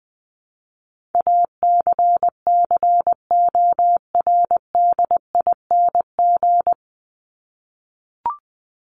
Morse code, heard as a keyed tone, about 20 wpm, 700 Hz. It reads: ACCORDING
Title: LY Suffix - Top 100 Words with Base Words 20wpm